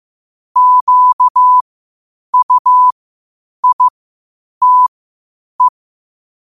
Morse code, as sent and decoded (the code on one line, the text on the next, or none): --.- ..- .. - .
QUITE